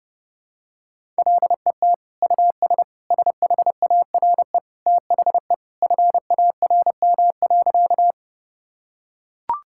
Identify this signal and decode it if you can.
Morse code, heard as a keyed tone, about 30 wpm, 700 Hz. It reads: LET US SHARE THE FARM.
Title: Sentences from Top 1000 Words - Review Part 6 - 30wpm